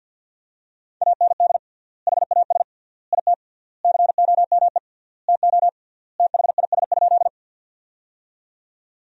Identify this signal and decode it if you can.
Morse code, heard as a keyed tone, about 52 wpm, 700 Hz. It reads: AND HAS IT COME TO THIS?
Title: 3x Word-Spacing - Sentences from Top 100 Words - 50wpm